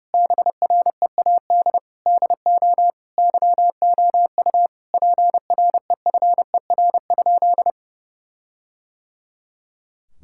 Morse code, heard as a keyed tone, about 30 wpm, 700 Hz. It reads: BREAD DO YOU PREFER?
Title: Sentences from Top 1600 Words 30wpm